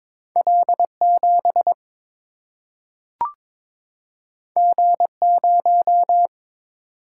Morse code, heard as a keyed tone, about 22 wpm, 700 Hz. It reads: L7 G0
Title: Single-Letter Single-Number 22wpm